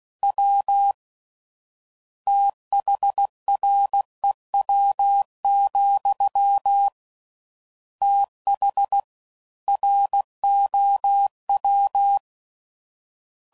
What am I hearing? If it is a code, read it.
Morse code, heard as a keyed tone, about 16 wpm, 800 Hz. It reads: W THREW, TH ROW
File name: Prefixes / FLs & THs / Fast TH